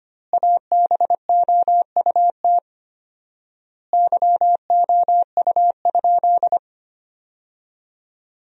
Morse code, heard as a keyed tone, about 25 wpm, 700 Hz. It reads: ABOUT YOU?
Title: 4x Word-Spacing - Sentences from Top 100 Words - 25wpm